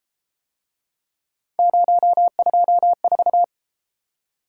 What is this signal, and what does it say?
Morse code, heard as a keyed tone, about 33 wpm, 700 Hz. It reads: 024